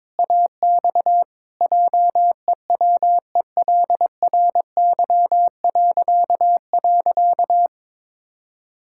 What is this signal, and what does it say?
Morse code, heard as a keyed tone, about 22 wpm, 700 Hz. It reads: AX JEWELRY..
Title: Pangram Challenge v2 - 22wpm